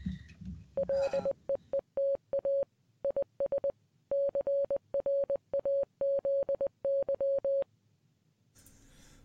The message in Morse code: LEETA IS CRAZY